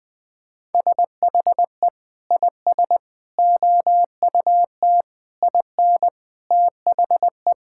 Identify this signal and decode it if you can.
Morse code, heard as a keyed tone, about 20 wpm, 700 Hz. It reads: SHE IS OUT IN THE
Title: Sentences with 3-Word Phrases - 20wpm